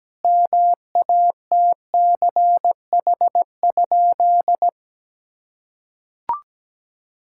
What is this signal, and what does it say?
Morse code, heard as a keyed tone, about 17 wpm, 700 Hz. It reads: MATCH?
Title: Sentences from Top 1600 Words 17wpm